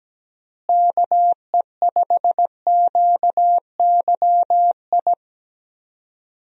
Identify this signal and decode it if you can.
Morse code, heard as a keyed tone, about 17 wpm, 700 Hz. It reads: KE5QYI
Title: USA Call Signs - 6 Characters - Repeated 2x - 17wpm